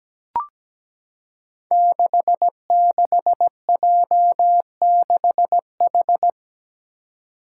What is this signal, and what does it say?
Morse code, heard as a keyed tone, about 17 wpm, 700 Hz. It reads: 66J6H
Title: Five Challenging Letter-Number Combinations 17wpm